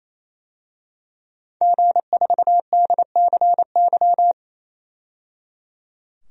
Morse code, heard as a keyed tone, about 28 wpm, 700 Hz. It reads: G4DCY
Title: DXCC-100 Call Signs - 5 Characters - 28wpm